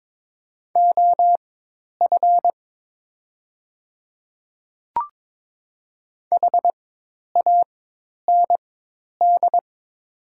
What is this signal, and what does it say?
Morse code, heard as a keyed tone, about 22 wpm, spaced as for 10 wpm, 700 Hz. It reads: OF HAND
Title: 4x Character Spacing - Top 500 Words - 22wpm